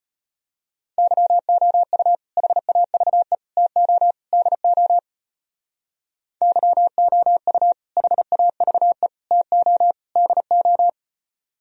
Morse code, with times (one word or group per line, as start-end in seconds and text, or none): 1.0-2.2 YOU
2.4-3.4 HAVE
3.6-4.1 TO
4.3-5.0 DO
6.4-7.7 YOU
8.0-9.1 HAVE
9.3-9.9 TO
10.2-10.9 DO